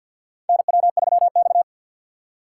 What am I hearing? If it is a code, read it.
Morse code, heard as a keyed tone, about 52 wpm, 700 Hz. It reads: NW3X
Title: USA Call Signs - 4 Characters - 2x Character Spacing - 50wpm